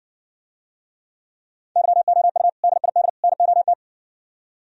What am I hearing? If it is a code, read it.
Morse code, heard as a keyed tone, about 66 wpm, 700 Hz. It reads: YOU DID NOT